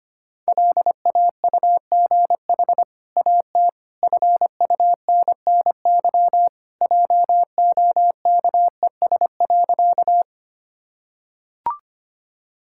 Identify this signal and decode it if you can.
Morse code, heard as a keyed tone, about 25 wpm, 700 Hz. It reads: LAUGH AT FUNNY JOKES.
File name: Easy Sentences with Hints - 25wpm